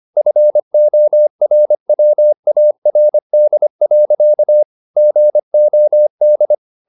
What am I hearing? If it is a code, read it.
Morse code, heard as a keyed tone, about 25 wpm, 600 Hz. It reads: FORWARD. GOD